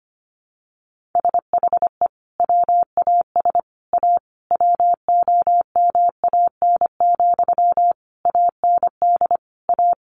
Morse code, heard as a keyed tone, about 25 wpm, 700 Hz. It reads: SHE WAS A WOMAN, AND A
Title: Sentences from Top 1400 Words 25wpm